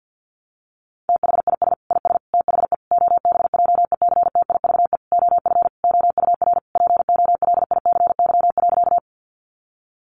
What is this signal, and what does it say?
Morse code, heard as a keyed tone, about 50 wpm, 700 Hz. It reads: THIS IS THE OBJECTIVE OF OUR POLICY.
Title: Sentences from Top 1200 Words 50wpm